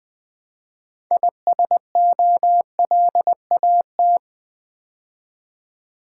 Morse code, heard as a keyed tone, about 21 wpm, 700 Hz. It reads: ISOLAT
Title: Build A Word - 2 Letters At A Time - 8-Letter Words - 20wpm